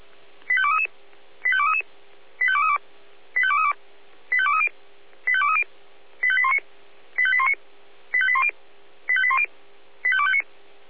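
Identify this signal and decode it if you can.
Five-tone selective call (ZVEI), heard as quick runs of five tones, tones of about 70 ms. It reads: 85233 85233 85232 85232 85230 85230 85819 85819 85810 85810 85259
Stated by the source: ZVEI five-tone selective call received with an SDR.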